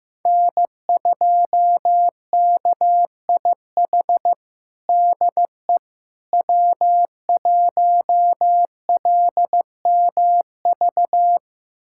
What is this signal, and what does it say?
Morse code, heard as a keyed tone, about 15 wpm, 700 Hz. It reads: N2KIH DE W1LMV
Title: USA Callsign DE Callsign - 5 Characters - 15wpm